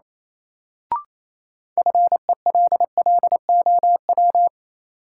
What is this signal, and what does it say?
Morse code, heard as a keyed tone, about 28 wpm, 700 Hz. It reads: FELLOW
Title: Sets of 5 Words Encore - New Word Drill - 28wpm